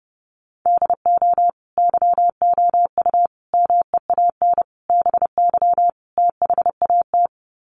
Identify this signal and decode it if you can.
Morse code, heard as a keyed tone, about 30 wpm, 700 Hz. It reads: DO YOU MEAN BY THAT
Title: Sets of 5 Words Encore2 30wpm